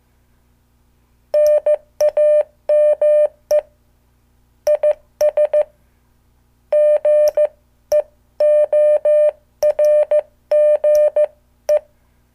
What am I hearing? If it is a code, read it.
Morse code, heard as a keyed tone, about 15 wpm, spaced as for 13 wpm, 600 Hz. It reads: NAME IS GEORGE
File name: morse / qso